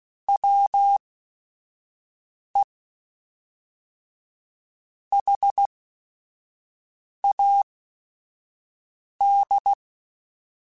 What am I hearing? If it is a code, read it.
Morse code, heard as a keyed tone, about 16 wpm, spaced as for 5 wpm, 800 Hz. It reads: WE HAD
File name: Weather Sets / Weather Sets-mp3s / Weather 4 28 @5